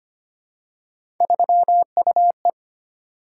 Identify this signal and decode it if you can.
Morse code, heard as a keyed tone, about 25 wpm, 700 Hz. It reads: 3UE
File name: Three Letter-Number Combinations - 25wpm